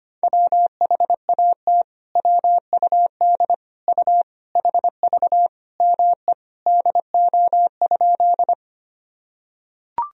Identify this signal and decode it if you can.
Morse code, heard as a keyed tone, about 25 wpm, 700 Hz. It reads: WHAT WUD U HV ME DO?
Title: Sentences from Top 200 Words and Abbreviations 25wpm